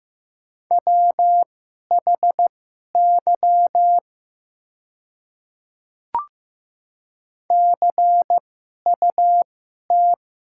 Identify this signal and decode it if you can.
Morse code, heard as a keyed tone, about 15 wpm, spaced as for 11 wpm, 700 Hz. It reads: WHY CUT
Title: 2x Character Spacing - Top 500 Words - 15wpm